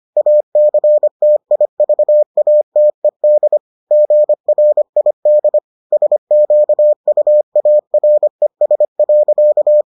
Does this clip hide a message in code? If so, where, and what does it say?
Yes: Morse code from 0.2 to 9.8 s, heard as a keyed tone, about 25 wpm, 600 Hz. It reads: ACTIVATED GRID SQUARES.